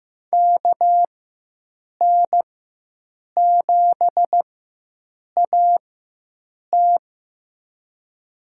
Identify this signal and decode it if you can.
Morse code, heard as a keyed tone, about 15 wpm, spaced as for 7 wpm, 700 Hz. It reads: KN7AT
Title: USA Call Signs - 5 Characters - 4x Character Spacing - 15wpm